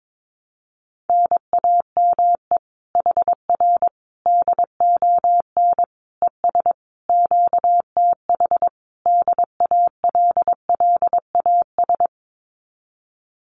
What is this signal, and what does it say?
Morse code, heard as a keyed tone, about 22 wpm, 700 Hz. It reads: NAME HR DON ES QTH DALLAS
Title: Common QSO Phrases Encore 22wpm